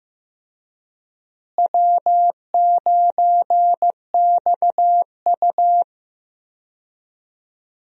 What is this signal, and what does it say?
Morse code, heard as a keyed tone, about 15 wpm, 700 Hz. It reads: W9XU
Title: USA Call Signs - 4 Characters - Repeated 2x - 15wpm